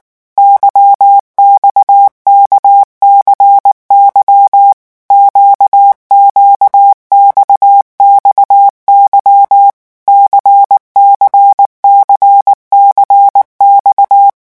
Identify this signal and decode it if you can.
Morse code, heard as a keyed tone, about 19 wpm, 800 Hz. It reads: YXKCY QQXXY CCCCX
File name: revision1-vitesse 18 mots